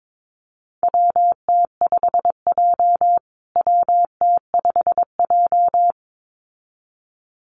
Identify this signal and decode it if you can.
Morse code, heard as a keyed tone, about 22 wpm, 700 Hz. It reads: WT5J WT5J
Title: USA Call Signs Repeated 2x 22wpm